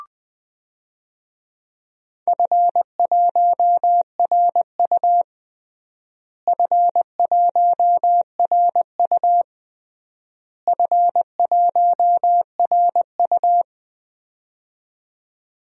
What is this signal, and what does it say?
Morse code, heard as a keyed tone, about 20 wpm, 700 Hz. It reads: F1RU F1RU F1RU
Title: DXCC-100 Call Signs - 4 Characters - Repeated 3x - 20wpm